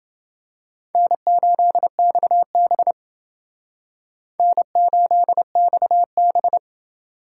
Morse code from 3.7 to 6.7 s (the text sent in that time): N8XB